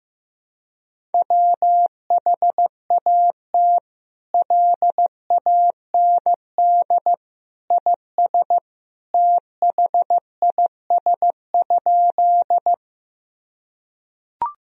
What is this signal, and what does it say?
Morse code, heard as a keyed tone, about 15 wpm, 700 Hz. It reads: WHAT LAND IS THIS?
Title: Sentences from Top 1000 Words - Review Part 1 - 15wpm